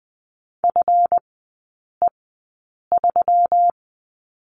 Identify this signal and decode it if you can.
Morse code, heard as a keyed tone, about 20 wpm, spaced as for 8 wpm, 700 Hz. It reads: FE3
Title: 166 Lesson 27e - Number 7 - Three Characters 20wpm